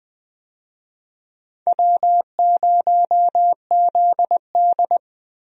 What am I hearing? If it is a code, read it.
Morse code, heard as a keyed tone, about 21 wpm, 700 Hz. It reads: W0ZD
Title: USA Call Signs - 4 Characters - Repeated 2x - 20wpm